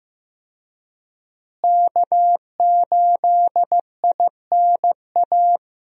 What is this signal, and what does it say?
Morse code, heard as a keyed tone, about 15 wpm, 700 Hz. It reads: K8INA
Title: USA Call Signs - 5 Characters - Repeated 2x - 15wpm